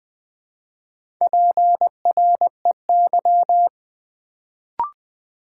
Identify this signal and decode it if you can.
Morse code, heard as a keyed tone, about 20 wpm, 700 Hz. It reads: PREY
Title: Build A Word - 1 Letter At A Time - 4-Letter Words - 20wpm